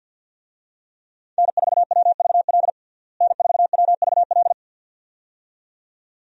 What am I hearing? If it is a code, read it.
Morse code, heard as a keyed tone, about 53 wpm, 700 Hz. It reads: N4WVL N4WVL
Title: USA Call Signs - 5 Characters - Repeated 2x - 50wpm